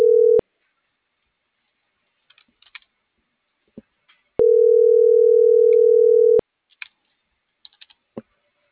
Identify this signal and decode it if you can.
Ringback tone (North America), 440 + 480 Hz.